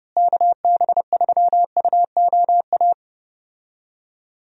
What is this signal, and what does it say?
Morse code, heard as a keyed tone, about 30 wpm, 700 Hz. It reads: KB3UOA
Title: USA Call Signs - 6 Characters - Repeated 2x - 30wpm